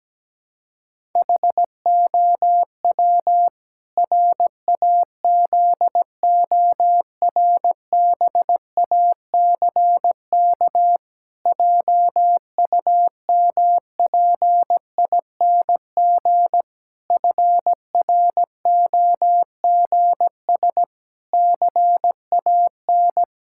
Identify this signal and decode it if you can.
Morse code, heard as a keyed tone, about 17 wpm, 700 Hz. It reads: HOW RAZORBACK JUMPING FROGS CAN
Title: Pangram Challenge v2 - 17wpm